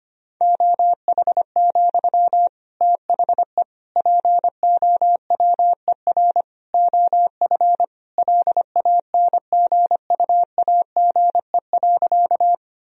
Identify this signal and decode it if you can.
Morse code, heard as a keyed tone, about 25 wpm, 700 Hz. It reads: OH, THE POWER OF LANGUAGE.
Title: Sentences from Top 1000 Words - Review Part 3 - 25wpm